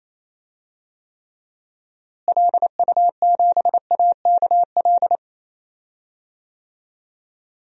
Morse code, heard as a keyed tone, about 28 wpm, 700 Hz. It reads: LU7AKL
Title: DXCC-100 Call Signs - 6 Characters - Repeated 3x - 28wpm